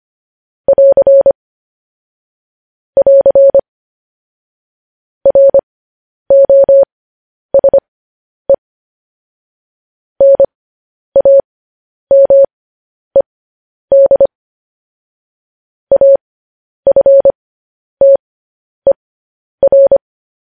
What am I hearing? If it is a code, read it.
Morse code, heard as a keyed tone, about 25 wpm, spaced as for 10 wpm, 550 Hz. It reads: <AR> <AR> ROSE NAMED AFTER
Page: news-headlines-in-morse-code-at-10-wpm